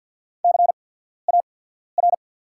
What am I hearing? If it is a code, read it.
Morse code, heard as a keyed tone, about 48 wpm, spaced as for 14 wpm, 690 Hz. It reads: CAR
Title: 8x Character Spacing - Top 300 Words - 50wpm